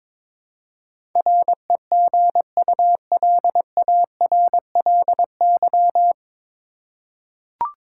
Morse code, heard as a keyed tone, about 22 wpm, 700 Hz. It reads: REGULARLY
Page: Morse Code Ninja practice 22wpm